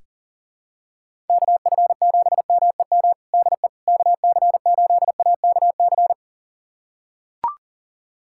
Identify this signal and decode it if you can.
Morse code, heard as a keyed tone, about 37 wpm, 700 Hz. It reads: KF7MEM DE KC8AKC
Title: USA Callsign DE Callsign - 6 Characters - 40wpm